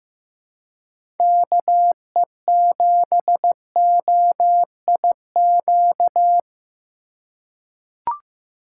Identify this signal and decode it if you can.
Morse code, heard as a keyed tone, about 15 wpm, 700 Hz. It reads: KE7OIQ
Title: USA Call Signs - 4x Character Spacing - 15wpm